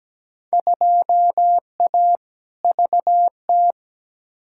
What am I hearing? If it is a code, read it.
Morse code, heard as a keyed tone, about 17 wpm, 700 Hz. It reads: 2A VT
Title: Field Day Exchanges Encore - 17wpm